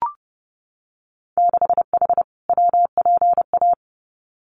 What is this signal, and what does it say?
Morse code, heard as a keyed tone, about 30 wpm, 700 Hz. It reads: 6H WPA
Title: Winter Field Day Exchanges Encore 30wpm